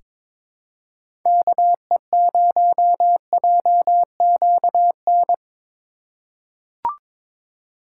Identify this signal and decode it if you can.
Morse code, heard as a keyed tone, about 22 wpm, 700 Hz. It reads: KE0JQN
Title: USA Call Signs - 8x Character Spacing - 22wpm